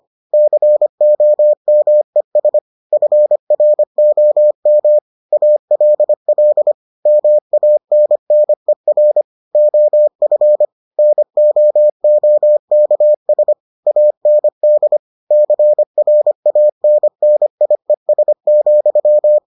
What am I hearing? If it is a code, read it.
Morse code, heard as a keyed tone, about 25 wpm, 600 Hz. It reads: COMES FROM ALL MANNER OF NOOKS AND CRANNIES,